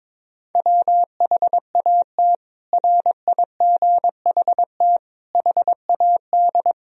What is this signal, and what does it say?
Morse code, heard as a keyed tone, about 22 wpm, 700 Hz. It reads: WHAT RIGHT HAD
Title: Sentences from Top 1000 Words - Review Part 1 - 22wpm